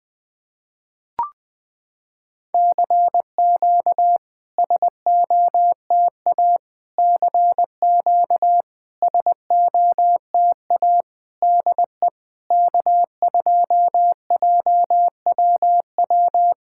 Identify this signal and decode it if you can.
Morse code, heard as a keyed tone, about 20 wpm, 700 Hz. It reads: CQ SOTA CQ SOTA DE K2JWW
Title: SOTA QSO Practice 20wpm